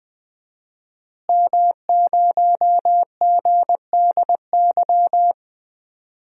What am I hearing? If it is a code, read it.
Morse code, heard as a keyed tone, about 20 wpm, 700 Hz. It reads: M0GDY
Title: DXCC-100 Call Signs - 5 Characters - Repeated 2x - 20wpm